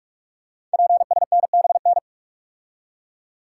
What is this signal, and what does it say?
Morse code, heard as a keyed tone, about 44 wpm, 690 Hz. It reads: PSNBN
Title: Code Groups - 5 Letters - 45wpm